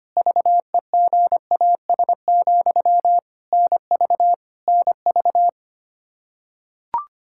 Morse code, heard as a keyed tone, about 25 wpm, 700 Hz. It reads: VEGAS, NV NV
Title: K1USN SST QSO Practice 25wpm